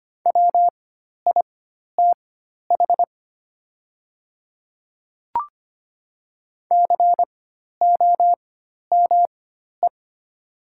Morse code, heard as a keyed tone, about 25 wpm, spaced as for 12 wpm, 690 Hz. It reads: WITH COME
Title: 4x Character Spacing - Top 100 Words - 25wpm